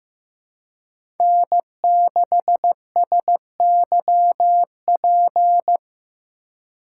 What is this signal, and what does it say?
Morse code, heard as a keyed tone, about 15 wpm, 700 Hz. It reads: N6SYP
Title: USA Call Signs - 5 Characters - 4x Character Spacing - 15wpm